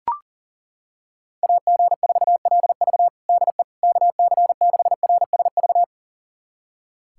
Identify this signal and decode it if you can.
Morse code, heard as a keyed tone, about 40 wpm, 700 Hz. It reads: AG4LV DE KC6RSV
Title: USA Callsign DE Callsign 40wpm